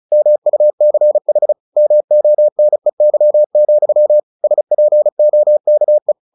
Morse code, heard as a keyed tone, about 34 wpm, 600 Hz. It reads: MUCH MONEY, SPOKE
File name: the old man and the sea / the old man and the sea 1 35wpm